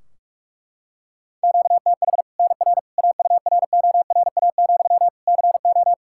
Morse code, heard as a keyed tone, about 41 wpm, 700 Hz. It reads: QTH NR AURORA, CO
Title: Practice QSO - Exchange 1 - Beginning - 45wpm